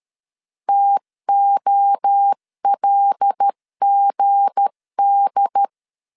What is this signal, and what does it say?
Morse code, heard as a keyed tone, about 13 wpm, 800 Hz. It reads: TOLGD
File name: LEZIONI / 7 GF9Z / 7 G 12wpm 2055